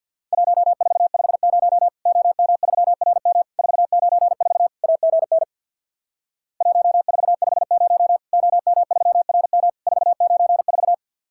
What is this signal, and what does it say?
Morse code, heard as a keyed tone, about 50 wpm, 700 Hz. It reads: 1450 OM3RM 494 AGN 1450 OM3RM 494
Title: Worked All Europe DX Contest - QTC Exchange with Repeats - 50wpm